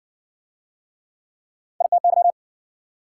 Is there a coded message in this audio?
Yes: Morse code from 1.8 to 2.3 s, heard as a keyed tone, about 60 wpm, 700 Hz. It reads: ITY